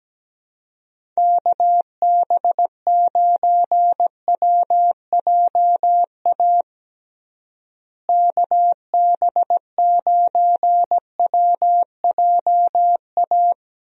Morse code, heard as a keyed tone, about 17 wpm, 700 Hz. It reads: KB9WJA KB9WJA
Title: USA Call Signs - 6 Characters - Repeated 3x - 17wpm